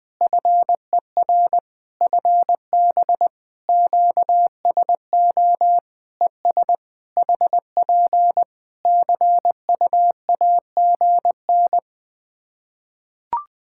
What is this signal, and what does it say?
Morse code, heard as a keyed tone, about 20 wpm, 700 Hz. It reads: FER FB QSO ES HP CUAGN
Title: Practice QSO - Exchange 4 - Ending - 20wpm